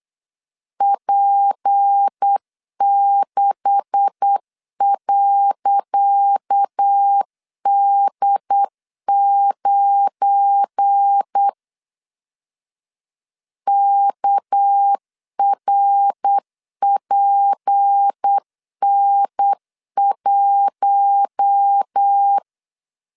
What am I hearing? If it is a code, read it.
Morse code, heard as a keyed tone, about 8 wpm, 800 Hz. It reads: P6.D9 KRPN1
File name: LEZIONI / 10 SIGNS / 10C DOT 8wpm 3649